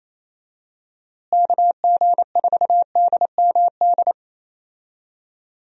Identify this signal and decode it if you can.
Morse code, heard as a keyed tone, about 27 wpm, 700 Hz. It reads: KG4DMD
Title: USA Call Signs - 6 Characters - Repeated 3x - 28wpm